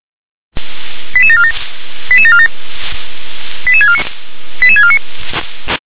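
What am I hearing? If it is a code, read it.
Five-tone selective call (ZVEI), heard as quick runs of five tones, tones of about 70 ms. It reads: 88747 88747 88649 88649
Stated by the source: Five-tone ZVEI selective call received on SDR.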